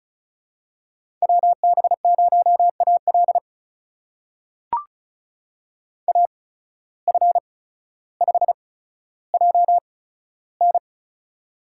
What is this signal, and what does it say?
Morse code, heard as a keyed tone, about 36 wpm, spaced as for 15 wpm, 700 Hz. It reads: WB0AL AF5JN
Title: USA Call Signs - 5 Characters - 8x Character Spacing - 35wpm